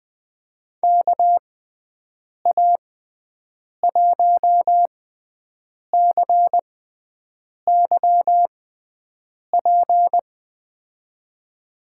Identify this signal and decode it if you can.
Morse code, heard as a keyed tone, about 20 wpm, spaced as for 7 wpm, 700 Hz. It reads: KA1CYP
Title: USA Call Signs - 6 Characters - 6x Character Spacing - 20wpm